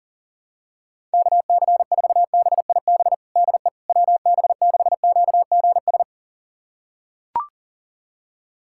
Morse code, heard as a keyed tone, about 40 wpm, 700 Hz. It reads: KC4BIB DE WB6QGS
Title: USA Callsign DE Callsign - 6 Characters - 40wpm